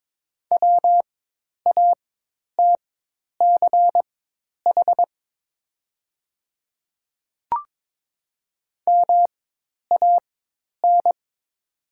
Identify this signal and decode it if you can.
Morse code, heard as a keyed tone, about 22 wpm, spaced as for 10 wpm, 700 Hz. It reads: WATCH MAN
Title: 4x Character Spacing - Top 300 Words - 22wpm